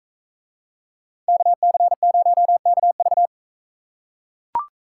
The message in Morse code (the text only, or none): KC0KV